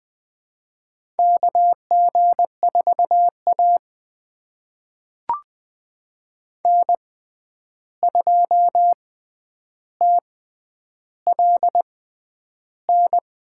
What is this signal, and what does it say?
Morse code, heard as a keyed tone, about 20 wpm, spaced as for 10 wpm, 700 Hz. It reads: KG4A N2TLN
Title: USA Call Signs - 6x Character Spacing - 20wpm